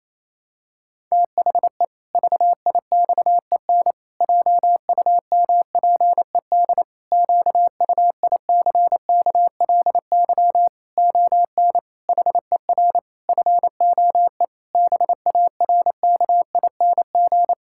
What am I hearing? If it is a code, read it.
Morse code, heard as a keyed tone, about 28 wpm, 700 Hz. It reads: THE VIXEN JUMPED QUICKLY ON HER FOE BARKING
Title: Pangram Challenge v2 - 28wpm